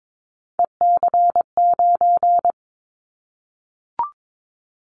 Morse code, heard as a keyed tone, about 22 wpm, 700 Hz. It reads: EC9
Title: Callsign Prefixes for Top 100 DXCC - 22wpm